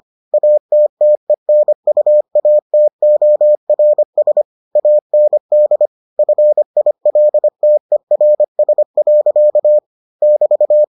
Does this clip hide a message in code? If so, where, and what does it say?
Yes: Morse code from 0.3 to 10.8 s, heard as a keyed tone, about 25 wpm, 600 Hz. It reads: ATTENUATORS AND FILTERS. <BT>